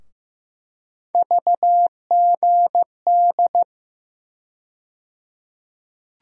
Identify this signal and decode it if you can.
Morse code, heard as a keyed tone, about 15 wpm, 700 Hz. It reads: VGD
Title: USA Call Sign Postfixes - 3 Letters 15wpm